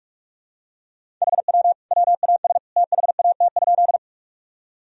Morse code, heard as a keyed tone, about 46 wpm, 700 Hz. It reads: HW WAS THAT?
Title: Sentences from Top 100 Words and Abbreviations 45wpm